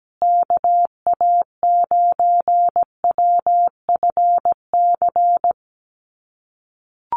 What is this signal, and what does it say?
Morse code, heard as a keyed tone, about 17 wpm, 700 Hz. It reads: KA9WFC
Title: USA Call Signs Repeated 2x 17wpm